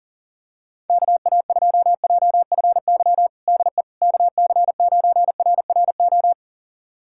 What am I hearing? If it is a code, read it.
Morse code, heard as a keyed tone, about 40 wpm, 700 Hz. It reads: KA2JFY DE KC9RRO
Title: USA Callsign DE Callsign - 6 Characters - 40wpm